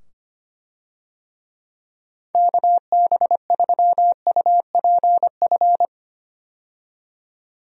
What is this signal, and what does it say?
Morse code, heard as a keyed tone, about 25 wpm, 700 Hz. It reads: KB3UPF
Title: USA Call Signs - 6 Characters - 25wpm